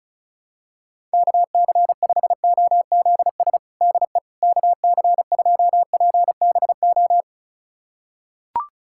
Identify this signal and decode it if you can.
Morse code, heard as a keyed tone, about 36 wpm, 700 Hz. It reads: KC5OZS DE KC2PBO
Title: USA Callsign DE Callsign 35wpm